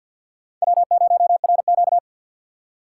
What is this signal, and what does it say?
Morse code, heard as a keyed tone, about 47 wpm, 700 Hz. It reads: W0RQ
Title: USA Call Signs - 4 Characters - 3x Character Spacing - 50wpm